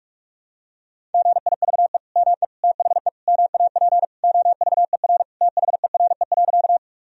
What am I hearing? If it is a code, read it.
Morse code, heard as a keyed tone, about 42 wpm, 700 Hz. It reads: GIVE ME THE MAP OVER THERE.
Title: Sentences from Top 1000 Words - Review Part 2 - 45wpm